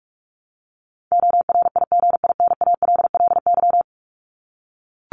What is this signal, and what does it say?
Morse code, heard as a keyed tone, about 44 wpm, 700 Hz. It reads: ORIGINALLY
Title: LY Suffix - Top 100 Words with Base Words 45wpm